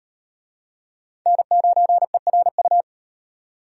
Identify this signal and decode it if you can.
Morse code, heard as a keyed tone, about 38 wpm, 700 Hz. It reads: N9ERU